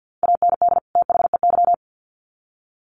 Morse code, heard as a keyed tone, about 50 wpm, 700 Hz. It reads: AND THEY
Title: Sets of 2 Words 50wpm